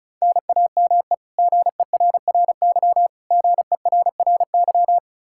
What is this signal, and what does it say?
Morse code, heard as a keyed tone, about 36 wpm, 700 Hz. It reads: NAME GERRY GERRY
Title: Practice QSO - Exchange 1 - Beginning - 35wpm